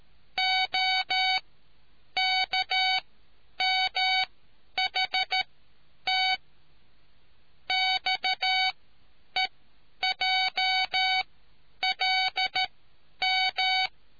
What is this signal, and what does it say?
Morse code, heard as a keyed tone, about 13 wpm, spaced as for 9 wpm, 750 Hz. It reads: OKMHT XEJLM